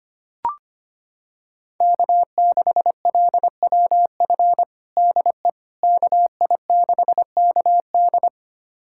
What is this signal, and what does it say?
Morse code, heard as a keyed tone, about 25 wpm, 700 Hz. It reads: K6LWF DE KI6KD
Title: USA Callsign DE Callsign - 5 Characters - 25wpm